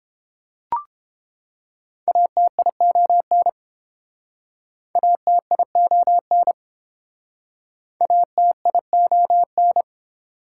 Morse code, heard as a keyed tone, about 31 wpm, 690 Hz. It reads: ATION ATION ATION